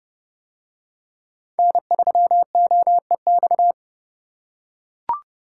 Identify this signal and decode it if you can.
Morse code, heard as a keyed tone, about 28 wpm, 700 Hz. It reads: N3OEX